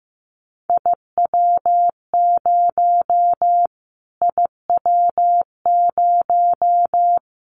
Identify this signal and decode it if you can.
Morse code, heard as a keyed tone, about 15 wpm, 700 Hz. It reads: IW0 IW0
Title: Callsign Prefixes for Top 25 DXCC - Repeated 3x - 15wpm